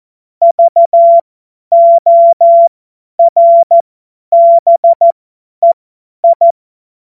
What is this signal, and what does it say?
Morse code, heard as a keyed tone, about 14 wpm, spaced as for 10 wpm, 680 Hz. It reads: VORBEI
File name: Morsekurs / Witze / Witz 21. Juni 2022 50 F70